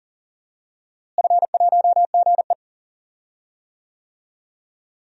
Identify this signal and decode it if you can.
Morse code, heard as a keyed tone, about 40 wpm, 700 Hz. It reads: F1GE
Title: DXCC-100 Call Signs - 4 Characters - Repeated 3x - 40wpm